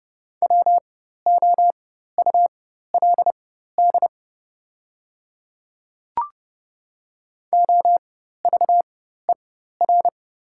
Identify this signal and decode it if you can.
Morse code, heard as a keyed tone, about 30 wpm, spaced as for 14 wpm, 690 Hz. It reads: WOULD OVER
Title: 4x Character Spacing - Top 100 Words - 30wpm